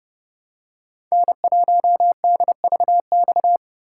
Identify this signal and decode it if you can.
Morse code, heard as a keyed tone, about 29 wpm, 700 Hz. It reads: N1DVX